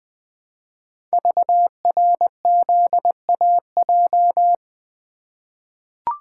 Morse code, heard as a keyed tone, about 20 wpm, 700 Hz. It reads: VRZAJ
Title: Code Groups - 5 Letters - 20wpm